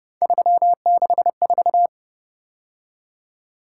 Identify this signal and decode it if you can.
Morse code, heard as a keyed tone, about 29 wpm, 700 Hz. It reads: 364
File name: Speed-Racing - Three Letter-Number Combinations - 25wpm